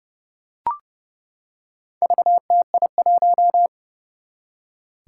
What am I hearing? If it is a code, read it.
Morse code, heard as a keyed tone, about 30 wpm, 700 Hz. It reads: VTI1
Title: Four Challenging Letter-Number Combinations 30wpm